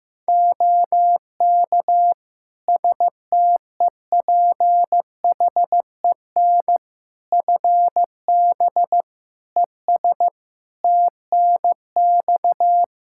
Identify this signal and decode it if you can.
Morse code, heard as a keyed tone, about 15 wpm, 700 Hz. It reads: OK STEPHEN FB ES TNX
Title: K1USN SST QSO Practice 15wpm